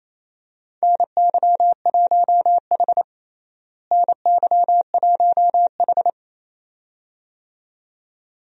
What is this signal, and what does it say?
Morse code, heard as a keyed tone, about 28 wpm, 700 Hz. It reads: NY1H NY1H
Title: USA Call Signs - 4 Characters - Repeated 2x - 28wpm